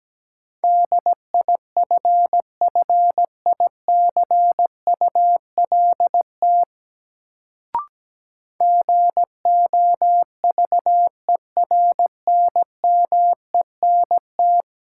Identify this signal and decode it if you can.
Morse code, heard as a keyed tone, about 17 wpm, 700 Hz. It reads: DIFFICULT GOVERNMENT
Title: Sentences from Top 600 Words - New Word Drill - 17wpm